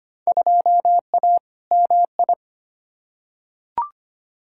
Morse code, heard as a keyed tone, about 25 wpm, 700 Hz. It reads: 2A MI
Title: Field Day Exchanges - 25wpm